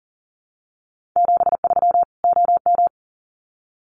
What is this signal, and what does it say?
Morse code, heard as a keyed tone, about 40 wpm, 700 Hz. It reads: 73 OM
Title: Common QSO Phrases 40wpm